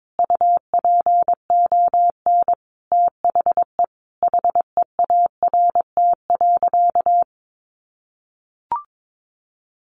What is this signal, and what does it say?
Morse code, heard as a keyed tone, about 22 wpm, 700 Hz. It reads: UPON THE HEART.
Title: Sentences from Top 1200 Words 22wpm